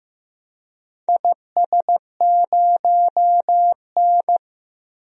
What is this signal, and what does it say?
Morse code, heard as a keyed tone, about 15 wpm, 700 Hz. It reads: IS0N
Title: DXCC-100 Call Signs - 4 Characters - Repeated 2x - 15wpm